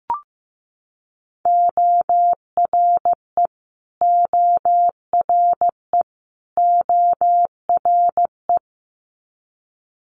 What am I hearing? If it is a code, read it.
Morse code, heard as a keyed tone, about 15 wpm, 700 Hz. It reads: ORE ORE ORE
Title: Multi-letter Phonemes Repeated 3x 15wpm